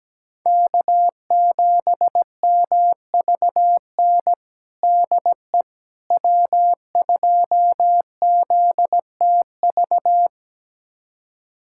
Morse code, heard as a keyed tone, about 17 wpm, 700 Hz. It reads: K7MVN DE W2ZTV
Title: USA Callsign DE Callsign 17wpm